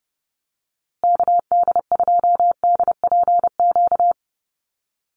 Morse code, heard as a keyed tone, about 30 wpm, 700 Hz. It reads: KD2DPQ
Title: USA Call Signs Repeated 2x 30wpm